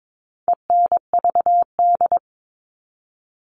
Morse code, heard as a keyed tone, about 22 wpm, 700 Hz. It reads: ENVD
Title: All Cut Numbers - 4-Digit 22wpm